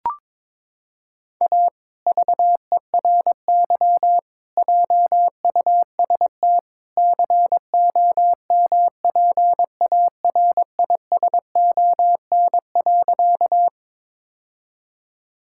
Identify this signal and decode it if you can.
Morse code, heard as a keyed tone, about 22 wpm, 700 Hz. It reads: A VERY JUST COMPARISON.
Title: Sentences from Top 1600 Words 22wpm